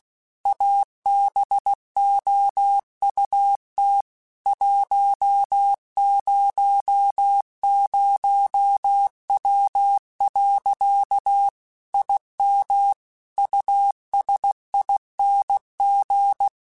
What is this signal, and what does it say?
Morse code, heard as a keyed tone, about 16 wpm, 800 Hz. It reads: ABOUT 100W. IM USING
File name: QSOs / QSOs-mp3s / WD0FBT 1@15